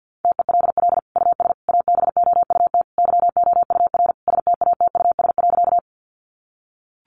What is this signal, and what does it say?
Morse code, heard as a keyed tone, about 50 wpm, 700 Hz. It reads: TELL US ABOUT YOUR STATUS.
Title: Sentences from Top 1400 Words 50wpm